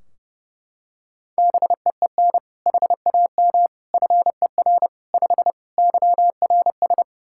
Morse code, heard as a keyed tone, about 30 wpm, 700 Hz. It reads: BEEN HAM FER 5 YRS
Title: Practice QSO - Exchange 3 - More Info - 30wpm